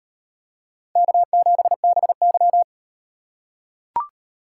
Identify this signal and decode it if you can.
Morse code, heard as a keyed tone, about 36 wpm, 700 Hz. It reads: K7BY